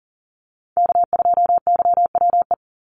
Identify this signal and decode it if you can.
Morse code, heard as a keyed tone, about 40 wpm, 700 Hz. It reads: K2YWE
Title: Call Signs Repeated 2x - 40wpm